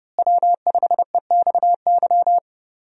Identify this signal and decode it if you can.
Morse code, heard as a keyed tone, about 30 wpm, 700 Hz. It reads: W5EXY
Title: USA Call Signs - 5 Characters - Repeated 2x - 30wpm